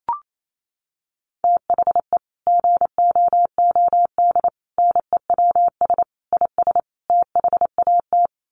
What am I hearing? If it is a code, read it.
Morse code, heard as a keyed tone, about 28 wpm, 700 Hz. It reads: THE GOOD NEWS IS THAT
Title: Sets of 5 Words 28wpm